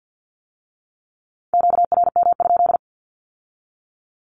Morse code, heard as a keyed tone, about 50 wpm, 700 Hz. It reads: QRM?
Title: Q-Codes with Spoken Meaning - 50wpm